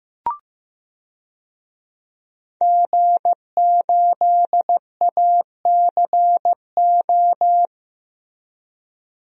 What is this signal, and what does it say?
Morse code, heard as a keyed tone, about 15 wpm, 700 Hz. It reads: G8ACO
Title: DXCC-100 Call Signs - 5 Characters - Repeated 3x - 15wpm